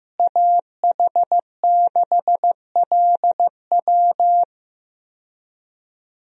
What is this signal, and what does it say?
Morse code, heard as a keyed tone, about 15 wpm, 700 Hz. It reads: AH6LW
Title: USA Call Signs - 5 Characters - Repeated 2x - 15wpm